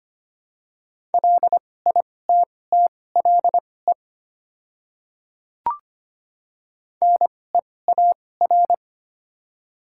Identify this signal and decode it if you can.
Morse code, heard as a keyed tone, about 25 wpm, spaced as for 18 wpm, 700 Hz. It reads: LITTLE NEAR
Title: 2x Character Spacing - Top 200 Words - 25wpm